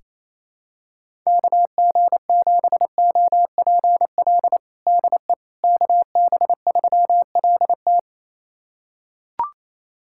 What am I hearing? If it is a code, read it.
Morse code, heard as a keyed tone, about 28 wpm, 700 Hz. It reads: KG7OPL DE KB3LT
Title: USA Callsign DE Callsign 28wpm